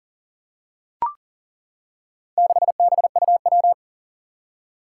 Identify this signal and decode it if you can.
Morse code, heard as a keyed tone, about 39 wpm, 700 Hz. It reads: 6BUW
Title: Four Challenging Letter-Number Combinations 40wpm